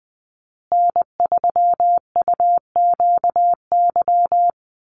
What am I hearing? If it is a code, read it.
Morse code, heard as a keyed tone, about 19 wpm, 700 Hz. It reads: N3UQY
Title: USA Call Signs Repeated 3x 20wpm